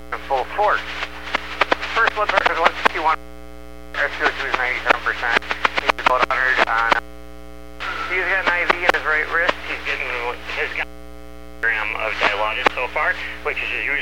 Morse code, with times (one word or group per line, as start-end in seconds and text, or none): none